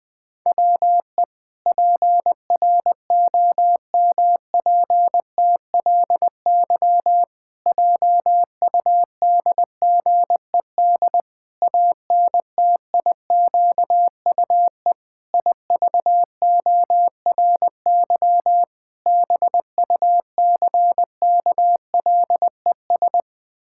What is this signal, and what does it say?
Morse code, heard as a keyed tone, about 20 wpm, 700 Hz. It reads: WE PROMPTLY JUDGED ANTIQUE IVORY BUCKLES
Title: Pangram Challenge v2 - 20wpm